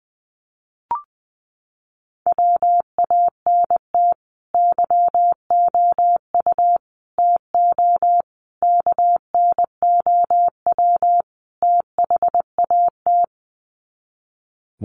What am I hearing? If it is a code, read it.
Morse code, heard as a keyed tone, about 20 wpm, 700 Hz. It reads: WANT YOU TO KNOW THAT
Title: Sets of 5 Words Encore 20wpm